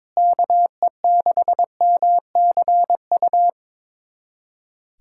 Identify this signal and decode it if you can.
Morse code, heard as a keyed tone, about 21 wpm, 700 Hz. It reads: KE6MCU
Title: USA Callsign DE Callsign 22wpm